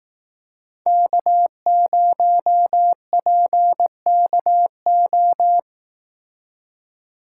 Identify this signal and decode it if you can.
Morse code, heard as a keyed tone, about 18 wpm, 700 Hz. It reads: K0PKO
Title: Speed Racing - USA Call Signs 15wpm